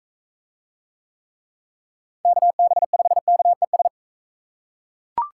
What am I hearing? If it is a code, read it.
Morse code, heard as a keyed tone, about 44 wpm, 700 Hz. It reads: KB5KES